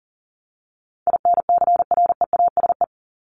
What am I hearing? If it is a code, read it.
Morse code, heard as a keyed tone, about 40 wpm, 700 Hz. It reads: INCREASE
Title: Top 501-750 Words 40wpm